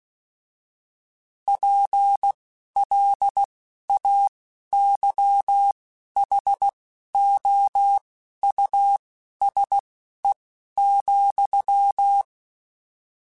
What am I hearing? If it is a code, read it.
Morse code, heard as a keyed tone, about 16 wpm, spaced as for 11 wpm, 800 Hz. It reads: PLAYHOUSE,
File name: General / General-mp3s / Houses @10